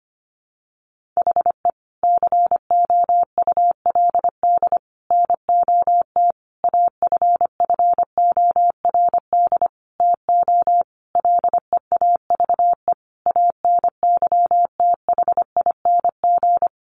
Morse code, heard as a keyed tone, about 25 wpm, 700 Hz. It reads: HE COULD NOT AFFORD TO LEAVE ANYTHING
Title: Sentences from Top 1100 Words 25wpm